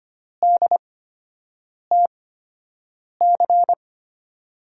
Morse code, heard as a keyed tone, about 25 wpm, spaced as for 7 wpm, 700 Hz. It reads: DTC
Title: USA Call Signs - 5 Characters - 8x Character Spacing - 25wpm